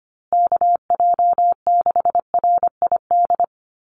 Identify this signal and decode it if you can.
Morse code, heard as a keyed tone, about 25 wpm, 700 Hz. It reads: KJ6RID
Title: USA Call Signs Repeated 3x 25wpm